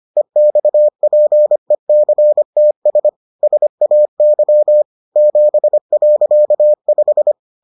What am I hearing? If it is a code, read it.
Morse code, heard as a keyed tone, about 25 wpm, 600 Hz. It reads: EXPECTS SAY 7.5